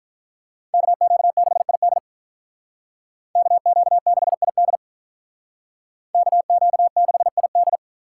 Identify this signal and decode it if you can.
Morse code, heard as a keyed tone, about 46 wpm, 700 Hz. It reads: KQ6ID KQ6ID KQ6ID